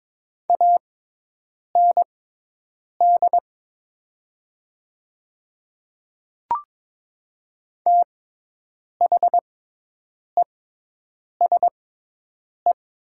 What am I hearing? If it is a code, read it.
Morse code, heard as a keyed tone, about 22 wpm, spaced as for 8 wpm, 700 Hz. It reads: AND THESE
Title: 6x Character Spacing - Top 100 Words - 22wpm